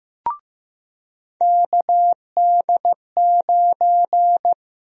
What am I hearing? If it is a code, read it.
Morse code, heard as a keyed tone, about 15 wpm, 700 Hz. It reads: KD9
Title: USA Call Sign Prefixes - Repeated 2x - 15wpm